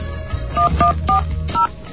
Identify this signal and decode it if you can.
DTMF, heard as telephone touch tones, tones of about 120 ms. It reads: 1140